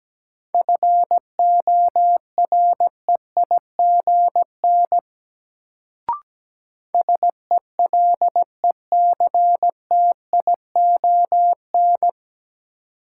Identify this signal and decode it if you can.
Morse code, heard as a keyed tone, about 17 wpm, 700 Hz. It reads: FOREIGN SELECTION
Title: Sentences from Top 1400 Words - New Word Drill Alt - 17wpm